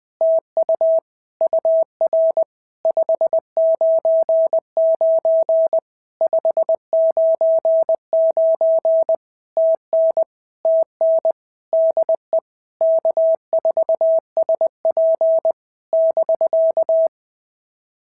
Morse code, heard as a keyed tone, about 20 wpm, 650 Hz. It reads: TU UR 599 599 TN TN DE K4SP <BK>
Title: POTA QSO Practice - 20wpm